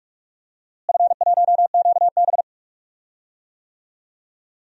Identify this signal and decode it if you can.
Morse code, heard as a keyed tone, about 46 wpm, 700 Hz. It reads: F1QB
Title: DXCC-100 Call Signs - 4 Characters - Repeated 2x - 45wpm